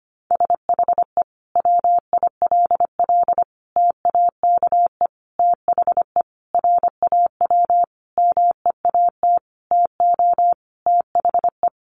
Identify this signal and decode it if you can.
Morse code, heard as a keyed tone, about 25 wpm, 700 Hz. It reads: SHE WILL TAKE THE RAW MEAT TO THE
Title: Sentences from Top 1500 Words 25wpm